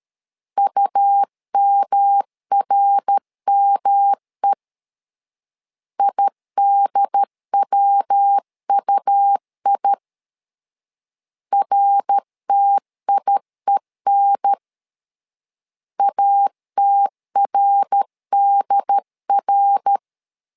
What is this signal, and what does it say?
Morse code, heard as a keyed tone, about 13 wpm, 800 Hz. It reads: UMRME IDWUI RTIEN ATRDR
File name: LEZIONI / 3 MWRO / 3 MWR 12wpm 1711